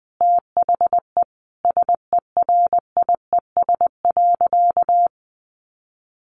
Morse code, heard as a keyed tone, about 20 wpm, 700 Hz. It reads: THE SERIES.
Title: Sentences from Top 1200 Words 20wpm